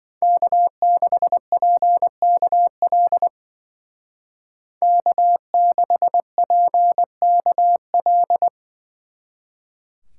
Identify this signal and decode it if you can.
Morse code, heard as a keyed tone, about 25 wpm, 700 Hz. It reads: K6PKL K6PKL
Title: Speed Racing - USA 1x3 Call Signs 20wpm